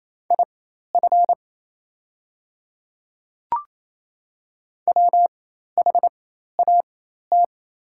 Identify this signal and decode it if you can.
Morse code, heard as a keyed tone, about 28 wpm, spaced as for 13 wpm, 700 Hz. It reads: IF WHAT
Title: 4x Character Spacing - Top 100 Words - 28wpm